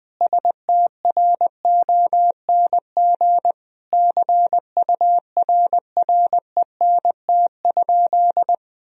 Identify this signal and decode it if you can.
Morse code, heard as a keyed tone, about 20 wpm, 700 Hz. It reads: STRONG CURRENT?
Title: Sentences from Top 1000 Words - Review Part 4 - 20wpm